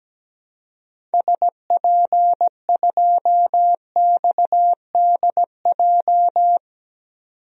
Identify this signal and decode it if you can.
Morse code, heard as a keyed tone, about 17 wpm, 700 Hz. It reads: SP2XDJ
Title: DXCC-100 Call Signs - 6 Characters - 17wpm